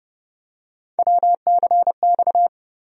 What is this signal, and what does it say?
Morse code, heard as a keyed tone, about 30 wpm, 700 Hz. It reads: WCX
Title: USA Call Sign Postfixes - 3 Letters 30wpm